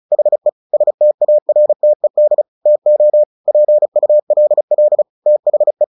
Morse code, heard as a keyed tone, about 35 wpm, 600 Hz. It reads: HE STARTED TO PULL THE
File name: the old man and the sea 2 35wpm